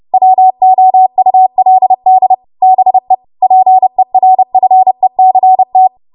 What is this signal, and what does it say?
Morse code, heard as a keyed tone, about 30 wpm, 750 Hz. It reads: WOULD BE PERFECT
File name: Morse / 30WPM